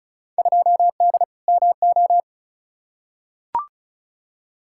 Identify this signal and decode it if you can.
Morse code, heard as a keyed tone, about 34 wpm, 700 Hz. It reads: 2D MO
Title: Field Day Exchanges - 35wpm